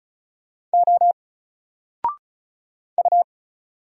Morse code, heard as a keyed tone, about 36 wpm, 700 Hz. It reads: O U
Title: Single Letter-Number Alt 35wpm